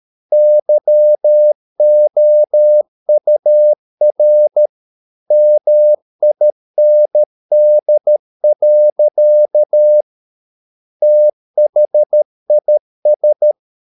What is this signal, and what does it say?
Morse code, heard as a keyed tone, about 13 wpm, 600 Hz. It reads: YOUR MIND. THIS